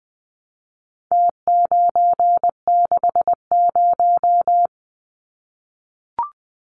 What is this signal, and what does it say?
Morse code, heard as a keyed tone, about 20 wpm, 700 Hz. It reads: T960
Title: Numbers - Contest Style - 4-Digit 20wpm